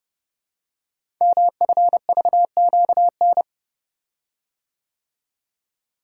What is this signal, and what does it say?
Morse code, heard as a keyed tone, about 30 wpm, 700 Hz. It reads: MFVQN
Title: Code Groups - 5 Letters - 30wpm